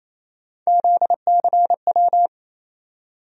Morse code, heard as a keyed tone, about 28 wpm, 700 Hz. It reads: ZCW
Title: USA Call Sign Postfixes - 3 Letters 28wpm